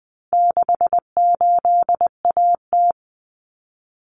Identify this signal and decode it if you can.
Morse code, heard as a keyed tone, about 20 wpm, 700 Hz. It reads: 68AT
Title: Common Cut Numbers - 4-Digit 20wpm